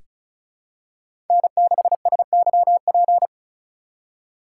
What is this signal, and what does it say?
Morse code, heard as a keyed tone, about 36 wpm, 700 Hz. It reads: N6SYP
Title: USA Call Signs - 5 Characters - 2x Character Spacing - 35wpm